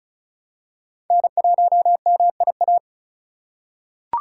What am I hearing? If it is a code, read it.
Morse code, heard as a keyed tone, about 36 wpm, 700 Hz. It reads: N1MIA
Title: USA Call Signs - 5 Characters - 2x Character Spacing - 35wpm